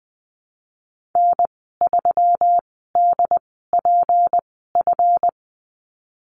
Morse code, heard as a keyed tone, about 20 wpm, spaced as for 15 wpm, 700 Hz. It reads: N3DPF
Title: 154 Lesson 25g - Letter p - Call Signs 20wpm